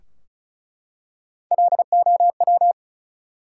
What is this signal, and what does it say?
Morse code, heard as a keyed tone, about 34 wpm, 700 Hz. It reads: LOW
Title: Three Letter Words - 35wpm